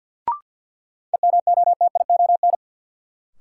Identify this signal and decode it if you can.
Morse code, heard as a keyed tone, about 52 wpm, 700 Hz. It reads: EMOTION
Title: Sentences from Top 1600 Words - New Word Drill Alt - 50wpm